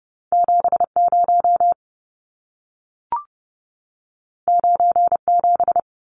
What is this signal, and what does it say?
Morse code, heard as a keyed tone, about 30 wpm, 700 Hz. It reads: 70 97
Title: Numbers - 2-Digit 30wpm